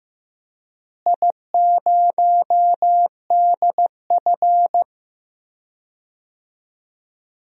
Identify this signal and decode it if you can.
Morse code, heard as a keyed tone, about 15 wpm, 700 Hz. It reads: I0DF
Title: DXCC-100 Call Signs - 4 Characters - Repeated 3x - 15wpm